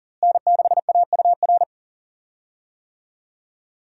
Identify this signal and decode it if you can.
Morse code, heard as a keyed tone, about 40 wpm, 700 Hz. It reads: N6AUR
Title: USA Call Signs - 5 Characters - Repeated 3x - 40wpm